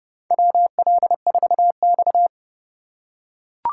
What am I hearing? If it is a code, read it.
Morse code, heard as a keyed tone, about 30 wpm, 700 Hz. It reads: WL4X
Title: USA Call Signs - 4 Characters - 3x Character Spacing - 30wpm